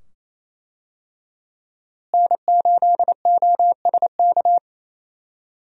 Morse code, heard as a keyed tone, about 28 wpm, 700 Hz. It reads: N8OSK
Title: USA Call Signs - 5 Characters - Repeated 2x - 28wpm